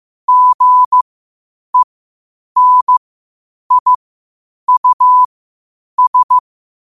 Morse code, heard as a keyed tone, about 15 wpm, spaced as for 8 wpm, 1000 Hz. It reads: GENIUS
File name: qotdfiles / 8wpm